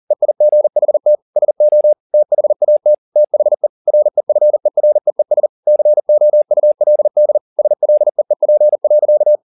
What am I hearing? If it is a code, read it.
Morse code, heard as a keyed tone, about 40 wpm, 600 Hz. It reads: EIGHT SO THAT THE REFEREES COULD SLEEP.
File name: the old man and the sea 2 40wpm